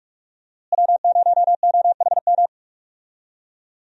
Morse code, heard as a keyed tone, about 46 wpm, 700 Hz. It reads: W0OHM
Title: USA Call Signs - 5 Characters - Repeated 2x - 45wpm